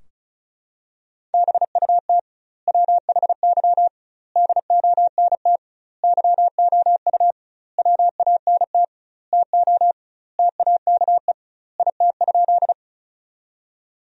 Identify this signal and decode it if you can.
Morse code, heard as a keyed tone, about 36 wpm, 700 Hz. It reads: BUT WHY DONT YOU WANT TO TAKE IT?
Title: 2x Word-Spacing - Sentences from Top 300 Words - 35wpm